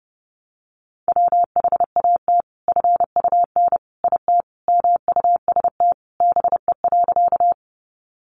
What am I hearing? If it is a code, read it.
Morse code, heard as a keyed tone, about 30 wpm, 700 Hz. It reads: WHAT FUN IT MUST BE.
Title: Sentences from Top 700 Words 30wpm